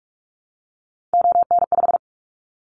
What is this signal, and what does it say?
Morse code, heard as a keyed tone, about 46 wpm, 700 Hz. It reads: ON5
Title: Callsign Prefixes for Top 100 DXCC - 45wpm